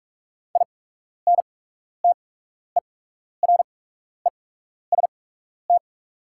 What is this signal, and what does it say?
Morse code, heard as a keyed tone, about 44 wpm, spaced as for 12 wpm, 700 Hz. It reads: INTEREST
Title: 8x Character Spacing - Top 500 Words - 45wpm